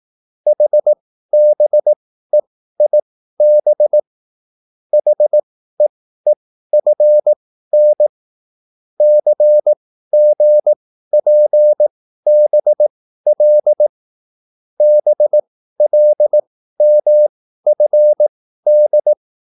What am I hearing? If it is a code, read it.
Morse code, heard as a keyed tone, about 18 wpm, spaced as for 13 wpm, 600 Hz. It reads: HBEIB HEEFN CGPBL BLMFD